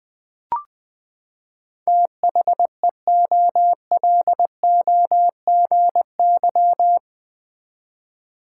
Morse code, heard as a keyed tone, about 20 wpm, 700 Hz. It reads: THEOLOGY
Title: Eight Letter Words - 20wpm